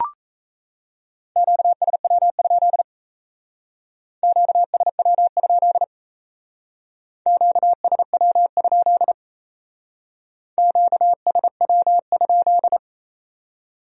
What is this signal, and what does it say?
Morse code, heard as a keyed tone, about 35 wpm, 700 Hz. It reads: QSW? QSW? QSW? QSW?